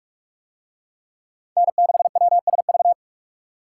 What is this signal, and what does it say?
Morse code, heard as a keyed tone, about 40 wpm, 700 Hz. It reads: N6WSV